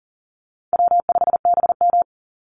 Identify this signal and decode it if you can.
Morse code, heard as a keyed tone, about 40 wpm, 700 Hz. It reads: W5BM
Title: Easy Call Signs - 40wpm